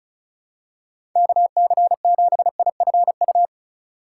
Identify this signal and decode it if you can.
Morse code, heard as a keyed tone, about 36 wpm, 700 Hz. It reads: KC7IFU
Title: USA Call Signs - 6 Characters - 2x Character Spacing - 35wpm